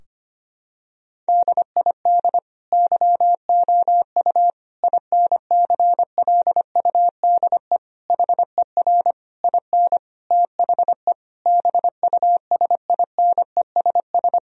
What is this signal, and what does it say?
Morse code, heard as a keyed tone, about 25 wpm, 700 Hz. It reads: DID YOU INCLUDE HER IN THE BUSINESS
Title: Sentences from Top 1000 Words - Review Part 3 - 25wpm